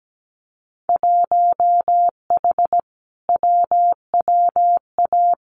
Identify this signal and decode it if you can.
Morse code, heard as a keyed tone, about 17 wpm, 700 Hz. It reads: 1H WWA
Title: Winter Field Day Exchanges Encore 17wpm